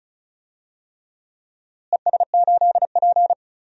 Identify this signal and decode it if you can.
Morse code, heard as a keyed tone, about 36 wpm, 700 Hz. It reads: ES8P
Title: DXCC-100 Call Signs - 4 Characters - Repeated 3x - 35wpm